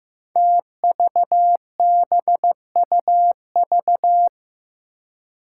Morse code, heard as a keyed tone, about 15 wpm, 700 Hz. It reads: TVBUV
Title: Five Challenging Letter-Number Combinations 15wpm